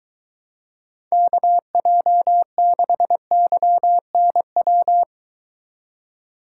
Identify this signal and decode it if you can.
Morse code, heard as a keyed tone, about 23 wpm, 700 Hz. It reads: KJ6YNW